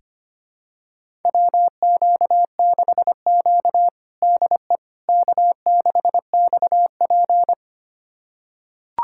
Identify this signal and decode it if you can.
Morse code, heard as a keyed tone, about 25 wpm, 700 Hz. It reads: WQ6Q DE K6XP
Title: USA Callsign DE Callsign - 4 Characters - 25wpm